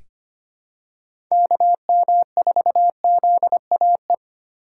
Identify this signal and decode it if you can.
Morse code, heard as a keyed tone, about 25 wpm, 700 Hz. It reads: KM4ZAE
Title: USA Call Signs - 6 Characters - 3x Character Spacing - 25wpm